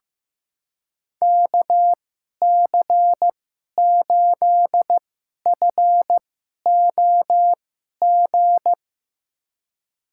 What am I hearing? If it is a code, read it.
Morse code, heard as a keyed tone, about 15 wpm, spaced as for 11 wpm, 700 Hz. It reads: KC8FOG
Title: USA Call Signs - 2x Character Spacing - 15wpm